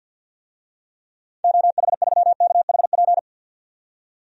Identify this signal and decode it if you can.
Morse code, heard as a keyed tone, about 51 wpm, 700 Hz. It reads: OH3KHP
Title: DXCC-100 Call Signs - 6 Characters - 50wpm